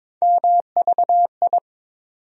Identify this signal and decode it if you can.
Morse code, heard as a keyed tone, about 22 wpm, 700 Hz. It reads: MVI
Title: USA Call Sign Postfixes - 3 Letters 22wpm